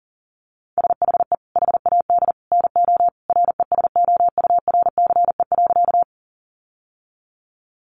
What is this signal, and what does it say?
Morse code, heard as a keyed tone, about 37 wpm, 700 Hz. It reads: SHE HAD NO RESOURCE.
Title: Sentences from Top 1300 Words 40wpm